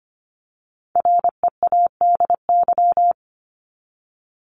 Morse code, heard as a keyed tone, about 25 wpm, 700 Hz. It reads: READY
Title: Top 500 Words 25wpm